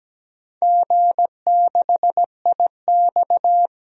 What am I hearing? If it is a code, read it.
Morse code, heard as a keyed tone, about 17 wpm, 700 Hz. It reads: G6IX
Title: DXCC-100 Call Signs - 4 Characters - 17wpm